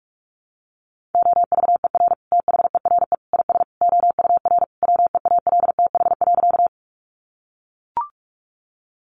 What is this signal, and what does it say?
Morse code, heard as a keyed tone, about 44 wpm, 700 Hz. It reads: OVER THERE IS OUR WEALTH.
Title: Sentences from Top 1200 Words 45wpm